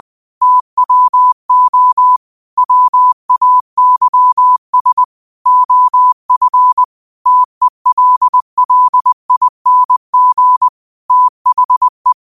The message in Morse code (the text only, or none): TWO WAYS OF TELLING THE